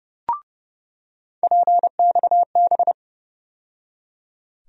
Morse code, heard as a keyed tone, about 30 wpm, 700 Hz. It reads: PXB
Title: USA Call Sign Postfixes - 3 Letters 30wpm